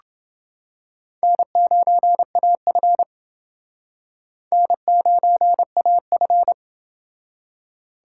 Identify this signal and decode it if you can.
Morse code, heard as a keyed tone, about 28 wpm, 700 Hz. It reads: N9AF N9AF